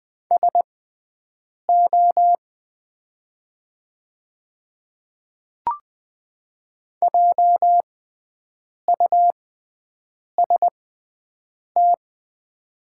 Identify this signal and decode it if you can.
Morse code, heard as a keyed tone, about 20 wpm, spaced as for 7 wpm, 700 Hz. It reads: SO JUST
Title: 6x Character Spacing - Top 200 Words - 20wpm